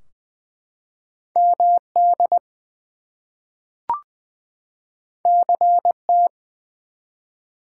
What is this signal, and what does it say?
Morse code, heard as a keyed tone, about 20 wpm, 700 Hz. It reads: MD CT
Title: US States and Canadian Provinces 20wpm